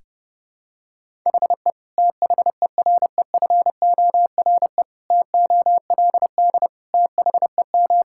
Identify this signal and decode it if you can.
Morse code, heard as a keyed tone, about 30 wpm, 700 Hz. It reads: HE THEREFORE TOLD THEM
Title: Sentences from Top 1000 Words - Review Part 5 - 30wpm